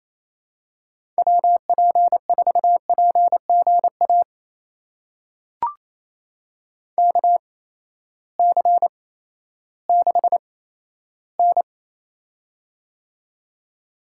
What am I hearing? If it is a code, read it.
Morse code, heard as a keyed tone, about 28 wpm, spaced as for 14 wpm, 700 Hz. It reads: WP4PGA KC6N
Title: USA Call Signs - 8x Character Spacing - 28wpm